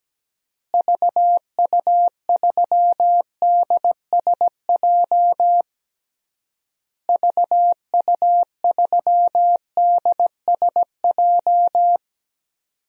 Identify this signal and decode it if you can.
Morse code, heard as a keyed tone, about 17 wpm, 700 Hz. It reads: VU3DSJ VU3DSJ
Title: DXCC-100 Call Signs - 6 Characters - Repeated 3x - 17wpm